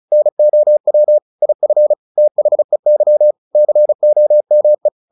Morse code, heard as a keyed tone, about 36 wpm, 600 Hz. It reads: NOW IF THEY COME
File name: the old man and the sea 2 35wpm